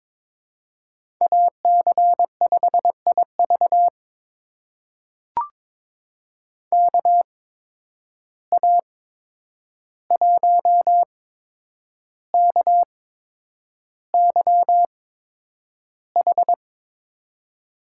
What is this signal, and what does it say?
Morse code, heard as a keyed tone, about 22 wpm, spaced as for 9 wpm, 700 Hz. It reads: AC5IV KA1KYH
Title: USA Call Signs - 8x Character Spacing - 22wpm